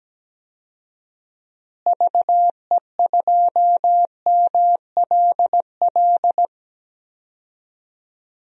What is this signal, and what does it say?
Morse code, heard as a keyed tone, about 17 wpm, 700 Hz. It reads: VE2MLL
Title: DXCC-100 Call Signs - 6 Characters - Repeated 2x - 17wpm